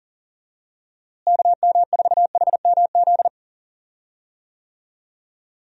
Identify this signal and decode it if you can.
Morse code, heard as a keyed tone, about 40 wpm, 700 Hz. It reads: KM4HMZ
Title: USA Call Signs - 6 Characters - Repeated 3x - 40wpm